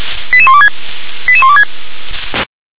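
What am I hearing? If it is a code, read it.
Five-tone selective call (ZVEI), heard as quick runs of five tones, tones of about 70 ms. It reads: 88126 88126
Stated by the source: Received on an SDR.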